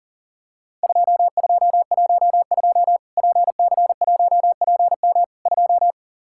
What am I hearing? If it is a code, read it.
Morse code, heard as a keyed tone, about 40 wpm, 700 Hz. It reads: 2212 PC1PM 2
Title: Worked All Europe DX Contest - QTC Exchange - 40wpm